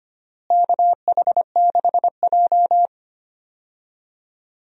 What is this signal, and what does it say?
Morse code, heard as a keyed tone, about 25 wpm, 700 Hz. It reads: KH6J
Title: USA Call Signs - 4 Characters - Repeated 3x - 25wpm